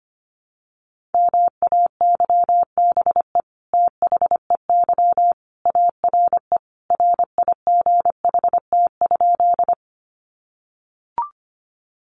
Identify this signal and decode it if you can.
Morse code, heard as a keyed tone, about 25 wpm, 700 Hz. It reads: MAYBE THEY ARE RIGHT?
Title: Sentences from Top 1000 Words 25wpm